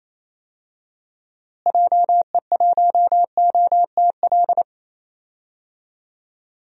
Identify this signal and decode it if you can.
Morse code, heard as a keyed tone, about 28 wpm, 700 Hz. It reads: JE1OTL
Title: DXCC-100 Call Signs - 6 Characters - Repeated 2x - 28wpm